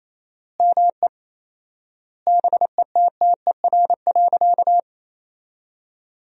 Morse code, heard as a keyed tone, about 28 wpm, 700 Hz. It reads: ME BETTER.
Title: 4x Word-Spacing - Sentences from Top 400 Words - 28wpm